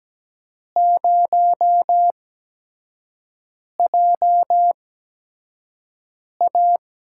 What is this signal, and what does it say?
Morse code, heard as a keyed tone, about 17 wpm, spaced as for 5 wpm, 700 Hz. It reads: 0JA
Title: USA Call Signs - 4 Characters - 8x Character Spacing - 17wpm